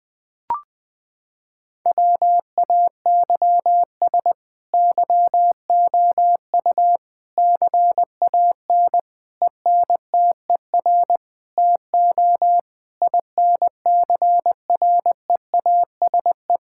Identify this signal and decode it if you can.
Morse code, heard as a keyed tone, about 20 wpm, 700 Hz. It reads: WAYS YOU CAN ENTER TO INCREASE
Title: Sets of 8 Words - 20wpm